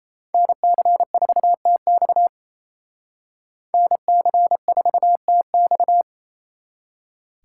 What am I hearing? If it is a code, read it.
Morse code, heard as a keyed tone, about 36 wpm, 700 Hz. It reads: NC4TX NC4TX